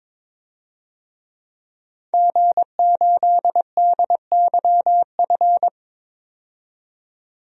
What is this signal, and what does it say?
Morse code, heard as a keyed tone, about 22 wpm, 700 Hz. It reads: G8DYF
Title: DXCC-100 Call Signs - 5 Characters - 22wpm